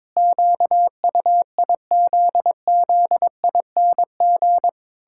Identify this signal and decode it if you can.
Morse code, heard as a keyed tone, about 21 wpm, 700 Hz. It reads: QUIZZING
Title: Pangram Challenge v2 - 22wpm